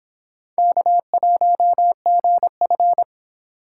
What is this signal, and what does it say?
Morse code, heard as a keyed tone, about 26 wpm, 700 Hz. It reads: K1GF